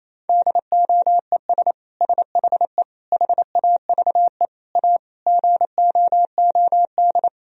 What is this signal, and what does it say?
Morse code, heard as a keyed tone, about 28 wpm, 700 Hz. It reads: DOES SHE HAVE A GOOD
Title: Sentences from Top 1000 Words - Review Part 2 - 28wpm